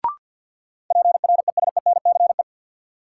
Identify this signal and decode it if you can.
Morse code, heard as a keyed tone, about 53 wpm, 700 Hz. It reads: PRESENCE
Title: Sets of 3 Words - New Word Drill - 50wpm